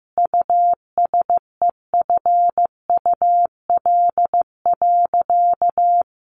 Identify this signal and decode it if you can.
Morse code, heard as a keyed tone, about 14 wpm, 700 Hz. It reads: USEFUL.
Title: Sentences from Top 1400 Words 15wpm